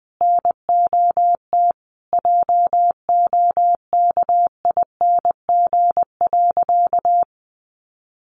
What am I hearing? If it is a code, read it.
Morse code, heard as a keyed tone, about 20 wpm, 700 Hz. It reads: NOT JOKING.
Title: A Sound Of Thunder 20wpm